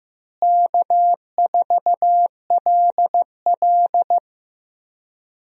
Morse code, heard as a keyed tone, about 15 wpm, 700 Hz. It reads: K4LL
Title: USA Call Signs - 4 Characters - Repeated 3x - 15wpm